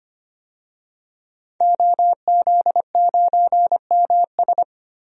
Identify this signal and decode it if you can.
Morse code, heard as a keyed tone, about 25 wpm, 700 Hz. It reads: OZ9MS
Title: DXCC-100 Call Signs - 5 Characters - 25wpm